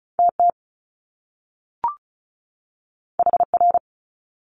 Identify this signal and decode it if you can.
Morse code, heard as a keyed tone, about 36 wpm, 700 Hz. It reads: TT HR
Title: Top Two Letter Combinations in Words 35wpm